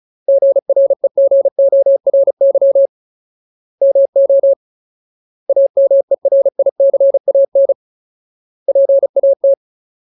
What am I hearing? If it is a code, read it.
Morse code, heard as a keyed tone, about 34 wpm, 550 Hz. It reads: GREGORY MO AMERICAN PAT
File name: Ultimate Word List / 35WPM Ultimate